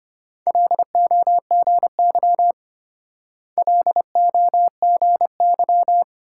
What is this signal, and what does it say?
Morse code, heard as a keyed tone, about 32 wpm, 700 Hz. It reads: LOGY LOGY